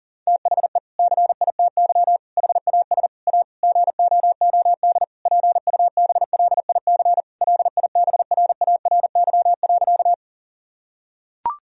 Morse code, heard as a keyed tone, about 40 wpm, 700 Hz. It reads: THE CITY HAS A GOOD PUBLIC LIBRARY.
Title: Sentences from Top 1000 Words - Review Part 4 - 40wpm